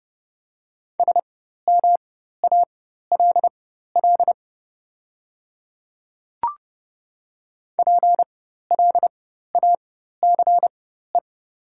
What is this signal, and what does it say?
Morse code, heard as a keyed tone, about 30 wpm, spaced as for 14 wpm, 690 Hz. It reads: SMALL PLACE
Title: 4x Character Spacing - Top 200 Words - 30wpm